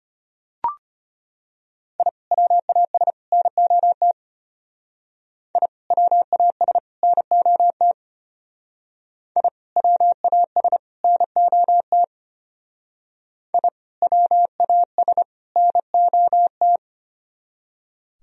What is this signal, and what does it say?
Morse code, heard as a keyed tone, about 32 wpm, 700 Hz. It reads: I WAS NOT I WAS NOT I WAS NOT I WAS NOT